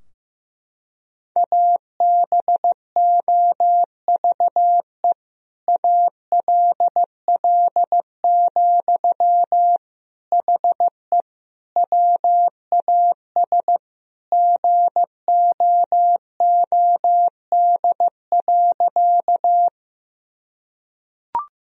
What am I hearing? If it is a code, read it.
Morse code, heard as a keyed tone, about 15 wpm, 700 Hz. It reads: ABOVE ALL, HE WAS GOOD.
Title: Sentences from Top 1000 Words - Review Part 2 - 15wpm